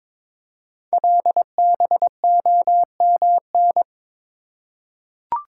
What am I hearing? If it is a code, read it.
Morse code, heard as a keyed tone, about 23 wpm, 700 Hz. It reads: LBOMN
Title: Code Groups - 5 Letters - 22wpm